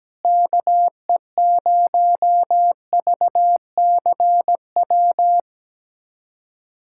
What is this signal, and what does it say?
Morse code, heard as a keyed tone, about 17 wpm, 700 Hz. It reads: KE0VCW
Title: USA Callsign DE Callsign - 6 Characters - 17wpm